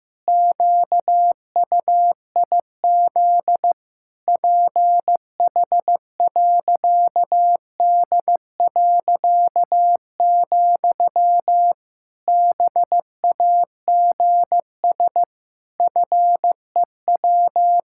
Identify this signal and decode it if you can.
Morse code, heard as a keyed tone, about 15 wpm, 700 Hz. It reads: QUIZ PH.D., BAGS FEW
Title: Pangram Challenge v2 - 15wpm